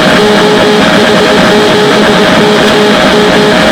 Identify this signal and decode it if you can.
Morse code, heard as a keyed tone, about 21 wpm, 420 Hz, with noise. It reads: OH7KM